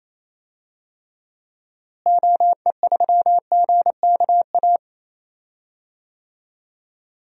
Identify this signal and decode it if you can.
Morse code, heard as a keyed tone, about 28 wpm, 700 Hz. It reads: OE3GKA
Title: DXCC-100 Call Signs - 6 Characters - Repeated 2x - 28wpm